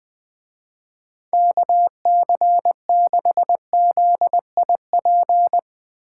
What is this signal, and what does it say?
Morse code, heard as a keyed tone, about 21 wpm, 700 Hz. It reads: KC6ZIP
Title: Speed Racing - USA 2x3 Call Signs 20wpm